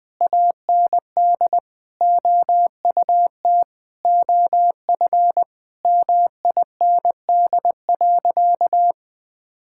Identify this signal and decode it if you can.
Morse code, heard as a keyed tone, about 20 wpm, 700 Hz. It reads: AND OUT OF MIND.
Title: Sentences with 7-Word Phrases - Spoken Beginning - 20wpm